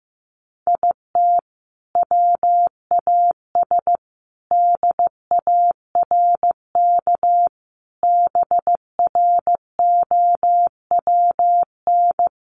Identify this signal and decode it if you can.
Morse code, heard as a keyed tone, about 14 wpm, 700 Hz. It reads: IT WAS DARK BROWN
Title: Sentences from Top 1400 Words 15wpm